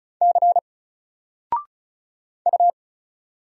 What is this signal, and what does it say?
Morse code, heard as a keyed tone, about 36 wpm, 700 Hz. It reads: C U
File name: Single-Letter-Number-35wpm